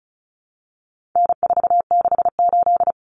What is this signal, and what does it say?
Morse code, heard as a keyed tone, about 34 wpm, 700 Hz. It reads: N468
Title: Common Cut Numbers - 4-Digit 35wpm